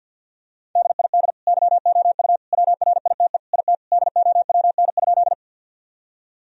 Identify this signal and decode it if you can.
Morse code, heard as a keyed tone, about 45 wpm, 700 Hz. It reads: DID YOU WRITE IT DOWN?
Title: Sentences from Top 1000 Words - Review Part 1 - 50wpm